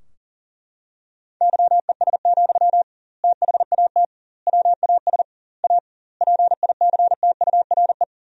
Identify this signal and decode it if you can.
Morse code, heard as a keyed tone, about 40 wpm, 700 Hz. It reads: YES, THAT WAS A PICTURE
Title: 2x Word-Spacing - Sentences from Top 500 Words - 40wpm